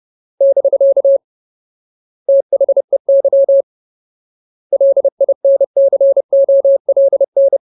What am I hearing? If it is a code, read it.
Morse code, heard as a keyed tone, about 30 wpm, 550 Hz. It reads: <BK> THEY LINCOLN
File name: Ultimate Word List / 30WPM Ultimate